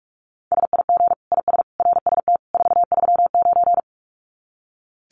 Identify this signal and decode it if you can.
Morse code, heard as a keyed tone, about 44 wpm, 700 Hz. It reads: SIG IS RST 439
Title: Common QSO Phrases 45wpm